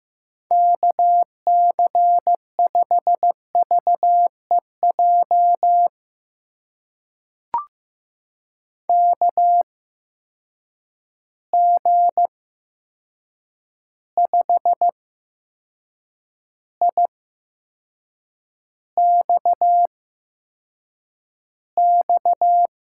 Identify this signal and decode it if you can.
Morse code, heard as a keyed tone, about 15 wpm, spaced as for 6 wpm, 700 Hz. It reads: KC5VEJ KG5IXX
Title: USA Call Signs - 6 Characters - 8x Character Spacing - 15wpm